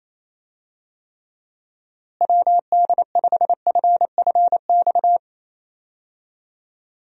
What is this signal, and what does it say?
Morse code, heard as a keyed tone, about 28 wpm, 700 Hz. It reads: WD5FFX
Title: USA Call Signs - 6 Characters - Repeated 3x - 28wpm